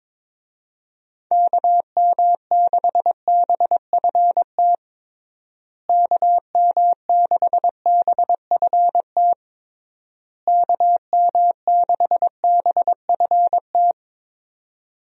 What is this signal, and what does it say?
Morse code, heard as a keyed tone, about 22 wpm, 700 Hz. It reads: KM6BFT KM6BFT KM6BFT
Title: USA Call Signs - 6 Characters - Repeated 3x - 22wpm